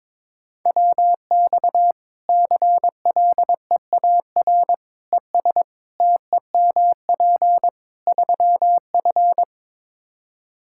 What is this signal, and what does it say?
Morse code, heard as a keyed tone, about 22 wpm, 700 Hz. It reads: WX CLEAR ES TEMP 3F
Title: Practice QSO - Exchange 2 - Info - 22wpm